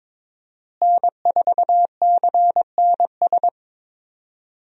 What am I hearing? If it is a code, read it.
Morse code, heard as a keyed tone, about 22 wpm, 700 Hz. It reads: N4CNS
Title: USA Call Signs - 5 Characters - Repeated 2x - 22wpm